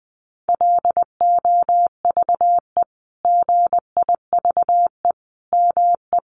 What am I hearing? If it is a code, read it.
Morse code, heard as a keyed tone, about 20 wpm, 700 Hz. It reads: LOVE GIVE ME
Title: Sentences from Top 1300 Words 20wpm